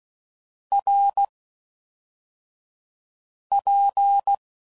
Morse code, heard as a keyed tone, about 16 wpm, spaced as for 4 wpm, 800 Hz. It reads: RP